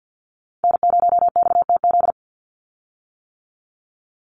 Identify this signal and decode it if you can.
Morse code, heard as a keyed tone, about 50 wpm, 700 Hz. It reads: N0XTZ
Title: USA Call Signs 50wpm